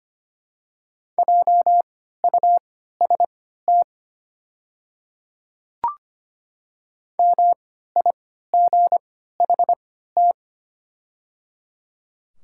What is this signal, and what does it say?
Morse code, heard as a keyed tone, about 25 wpm, spaced as for 14 wpm, 700 Hz. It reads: JUST MIGHT
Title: 3x Character Spacing - Top 500 Words - 25wpm